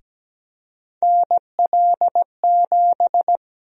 Morse code, heard as a keyed tone, about 17 wpm, 700 Hz. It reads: NL7
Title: USA Call Sign Prefixes - 17wpm